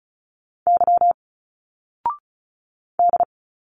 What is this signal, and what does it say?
Morse code, heard as a keyed tone, about 36 wpm, 700 Hz. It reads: Y D
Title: Single Letters 35wpm